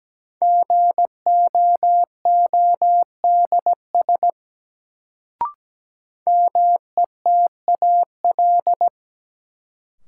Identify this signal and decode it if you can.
Morse code, heard as a keyed tone, about 17 wpm, 700 Hz. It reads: GOODS METAL
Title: Sentences from Top 1000 Words - Review Part 6 - New Word Drill - 17wpm